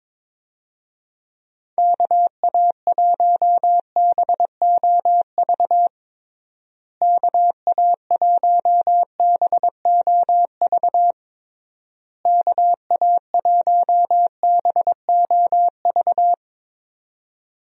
Morse code, heard as a keyed tone, about 22 wpm, 700 Hz. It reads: KA1BOV KA1BOV KA1BOV
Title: USA Call Signs - 6 Characters - Repeated 3x - 22wpm